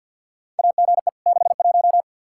Morse code, heard as a keyed tone, about 52 wpm, 700 Hz. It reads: AGE 61
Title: Practice QSO - Exchange 3 - More Info - 50wpm